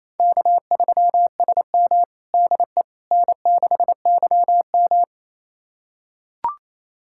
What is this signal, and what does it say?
Morse code, heard as a keyed tone, about 28 wpm, 700 Hz. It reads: K3SM DE N6YM
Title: USA Callsign DE Callsign - 4 Characters - 28wpm